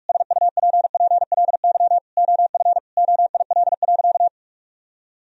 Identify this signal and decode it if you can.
Morse code, heard as a keyed tone, about 41 wpm, 700 Hz. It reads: SUPPLY OF OIL.
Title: Sentences from Top 1000 Words - Review Part 4 - 45wpm